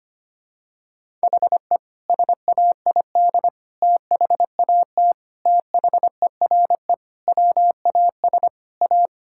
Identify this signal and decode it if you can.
Morse code, heard as a keyed tone, about 25 wpm, 700 Hz. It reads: HE SAID THAT THERE WAS A
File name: practice / 25wpm / Sentences with 3-Word Phrases - 25wpm